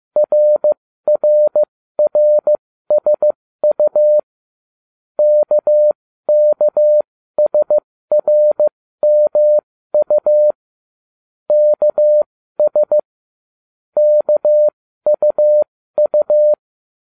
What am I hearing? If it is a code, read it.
Morse code, heard as a keyed tone, about 15 wpm, spaced as for 13 wpm, 600 Hz. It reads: RRRSU KKSRMU KS KUU